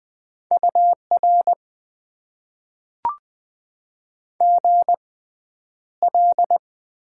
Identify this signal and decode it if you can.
Morse code, heard as a keyed tone, about 20 wpm, spaced as for 10 wpm, 700 Hz. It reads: UR GL
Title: Top 90 QSO Elements - 6x Character Spacing - 20wpm